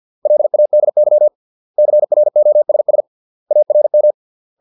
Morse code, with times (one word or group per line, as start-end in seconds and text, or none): 0.3-1.3 LADY
1.8-3.0 CROSS
3.5-4.1 ARM